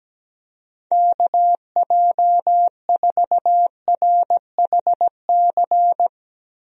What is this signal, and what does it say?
Morse code, heard as a keyed tone, about 17 wpm, 700 Hz. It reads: KJ4RHC
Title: USA Call Signs - 6 Characters - Repeated 2x - 17wpm